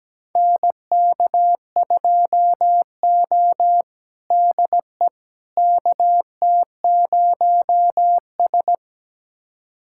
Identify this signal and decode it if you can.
Morse code, heard as a keyed tone, about 17 wpm, 700 Hz. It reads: NK2O DE KT0S
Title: USA Callsign DE Callsign - 4 Characters - 17wpm